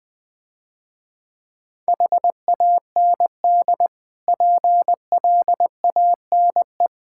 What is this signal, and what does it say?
Morse code, heard as a keyed tone, about 22 wpm, 700 Hz. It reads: HAND PLANE
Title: Difficult Words - Speed-Racing with Hints - 20wpm